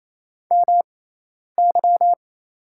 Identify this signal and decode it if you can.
Morse code, heard as a keyed tone, about 28 wpm, spaced as for 10 wpm, 700 Hz. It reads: MY
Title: 6x Character Spacing - Top 100 Words - 28wpm